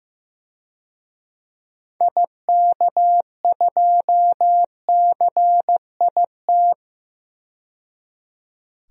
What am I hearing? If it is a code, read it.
Morse code, heard as a keyed tone, about 15 wpm, 700 Hz. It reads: IK2CIT
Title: DXCC-100 Call Signs - 6 Characters - 15wpm